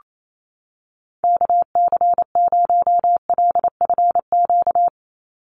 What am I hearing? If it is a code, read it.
Morse code, heard as a keyed tone, about 28 wpm, 700 Hz. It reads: KC0LFQ
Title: USA Call Signs 28wpm